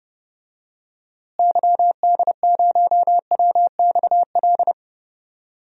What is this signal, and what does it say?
Morse code, heard as a keyed tone, about 30 wpm, 700 Hz. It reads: YD0WXL
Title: DXCC-100 Call Signs - 6 Characters - Repeated 2x - 30wpm